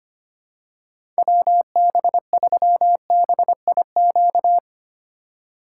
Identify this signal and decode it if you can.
Morse code, heard as a keyed tone, about 25 wpm, 700 Hz. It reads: WB3BIQ
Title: K1USN SST QSO Practice - 25wpm